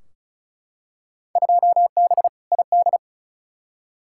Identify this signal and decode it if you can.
Morse code, heard as a keyed tone, about 36 wpm, 690 Hz. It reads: 2B ID
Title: Field Day Exchanges Encore - 35wpm